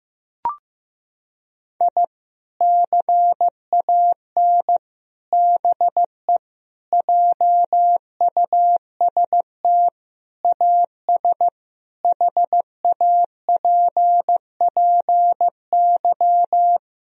Morse code, heard as a keyed tone, about 15 wpm, 700 Hz. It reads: I CAN BE JUST AS HAPPY
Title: Sentences from Top 1000 Words - Review Part 3 - 15wpm